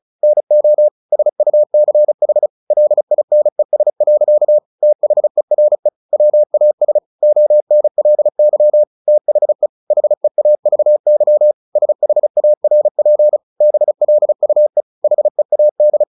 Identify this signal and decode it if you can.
Morse code, heard as a keyed tone, about 34 wpm, 600 Hz. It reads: NO SUCH LINES. THERE WAS ONLY THE HEAVY SHARP BLUE HEAD
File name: the old man and the sea 2 35wpm